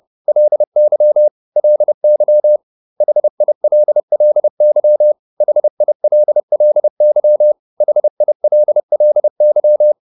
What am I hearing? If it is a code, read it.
Morse code, heard as a keyed tone, about 30 wpm, 600 Hz. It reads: LY LY HILLY HILLY HILLY